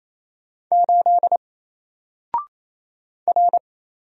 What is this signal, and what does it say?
Morse code, heard as a keyed tone, about 27 wpm, 700 Hz. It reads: 8 R
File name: Single-Letter-Number-28wpm